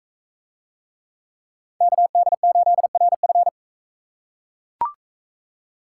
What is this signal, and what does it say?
Morse code, heard as a keyed tone, about 42 wpm, 700 Hz. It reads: KD8RF